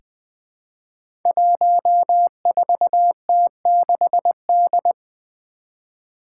Morse code, heard as a keyed tone, about 20 wpm, 700 Hz. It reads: 14T6D
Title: Five Challenging Letter-Number Combinations 20wpm